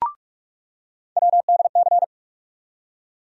Morse code, heard as a keyed tone, about 43 wpm, 700 Hz. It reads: WDC
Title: USA Call Sign Postfixes - 3 Letters 45wpm